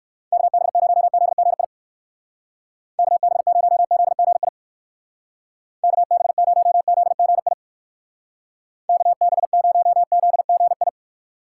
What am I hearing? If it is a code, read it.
Morse code, heard as a keyed tone, about 59 wpm, 700 Hz. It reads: KB0ZGI KB0ZGI KB0ZGI KB0ZGI